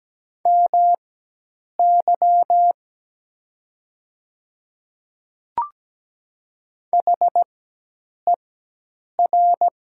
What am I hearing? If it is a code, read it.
Morse code, heard as a keyed tone, about 17 wpm, spaced as for 8 wpm, 700 Hz. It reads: MY HER
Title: 4x Character Spacing - Top 300 Words - 17wpm